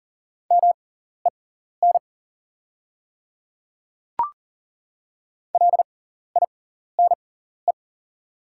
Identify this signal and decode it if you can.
Morse code, heard as a keyed tone, about 41 wpm, spaced as for 14 wpm, 700 Hz. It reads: MEN LINE
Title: 6x Character Spacing - Top 200 Words - 40wpm